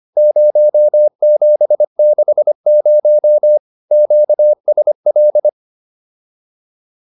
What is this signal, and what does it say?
Morse code, heard as a keyed tone, about 25 wpm, 600 Hz. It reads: 0760 QSL